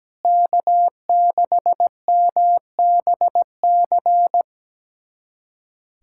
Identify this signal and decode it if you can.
Morse code, heard as a keyed tone, about 17 wpm, 700 Hz. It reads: K6MBC
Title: USA Callsign DE Callsign - 5 Characters - 17wpm